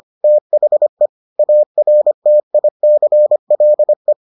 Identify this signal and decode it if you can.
Morse code, heard as a keyed tone, about 25 wpm, 600 Hz. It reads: THE ARTICLE